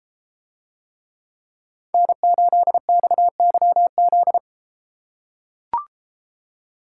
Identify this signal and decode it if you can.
Morse code, heard as a keyed tone, about 33 wpm, 700 Hz. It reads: N8XYZ